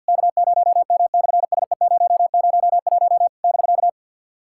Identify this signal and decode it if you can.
Morse code, heard as a keyed tone, about 50 wpm, 700 Hz. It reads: K0M/SE001 <BK>
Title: SOTA QSO Practice 50wpm